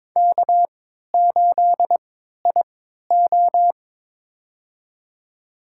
Morse code, heard as a keyed tone, about 22 wpm, spaced as for 13 wpm, 700 Hz. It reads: K8IO
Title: USA Call Signs - 4 Characters - 3x Character Spacing - 22wpm